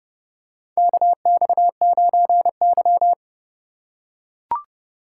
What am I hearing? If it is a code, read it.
Morse code, heard as a keyed tone, about 30 wpm, 700 Hz. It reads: KX9Y
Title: USA Call Signs - 4 Characters - 6x Character Spacing - 30wpm